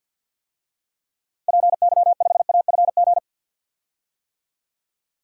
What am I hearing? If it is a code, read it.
Morse code, heard as a keyed tone, about 51 wpm, 700 Hz. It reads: PY5AFG
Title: DXCC-100 Call Signs - 6 Characters - 50wpm